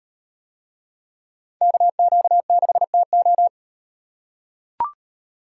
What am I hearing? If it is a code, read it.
Morse code, heard as a keyed tone, about 36 wpm, 700 Hz. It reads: KQ6TO